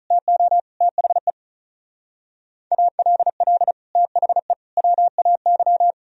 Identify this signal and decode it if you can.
Morse code, heard as a keyed tone, about 42 wpm, 700 Hz. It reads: TO THE ALL THE WAY
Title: Speed-Racing - Sets of 5 Words 35wpm